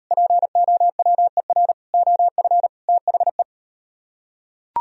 Morse code, heard as a keyed tone, about 38 wpm, 700 Hz. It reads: POWER OF THE